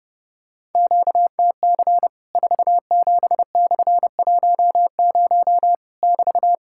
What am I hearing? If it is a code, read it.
Morse code, heard as a keyed tone, about 30 wpm, 700 Hz. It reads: QTC 47/10 <BT>
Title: Worked All Europe DX Contest - QTC Exchange - 30wpm